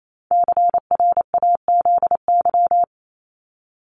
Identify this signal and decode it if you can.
Morse code, heard as a keyed tone, about 28 wpm, 700 Hz. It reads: CRAZY
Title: Top 1501-1750 Words 28wpm